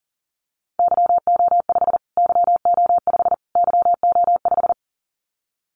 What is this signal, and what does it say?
Morse code, heard as a keyed tone, about 40 wpm, 700 Hz. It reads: YO5 YO5 YO5
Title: Callsign Prefixes for Top 25 DXCC - Repeated 3x - 40wpm